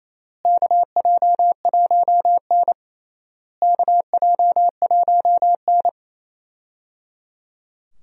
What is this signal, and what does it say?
Morse code, heard as a keyed tone, about 28 wpm, 700 Hz. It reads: KJ1N KJ1N
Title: USA Call Signs - 4 Characters - Repeated 3x - 28wpm